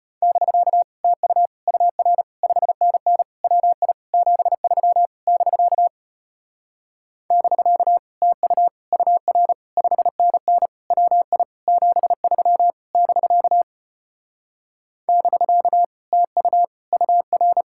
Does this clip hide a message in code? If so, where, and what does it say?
Yes: Morse code from 0.2 to 17.6 s, heard as a keyed tone, about 33 wpm, 700 Hz. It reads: <BK> TU UR 5NN WI 73 <BK> <BK> TU UR 5NN WI 73 <BK> <BK> TU UR